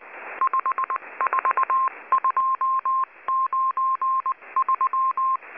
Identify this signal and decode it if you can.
Morse code, heard as a keyed tone, about 19 wpm, 1060 Hz. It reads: 54293
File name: HM02- -0540utc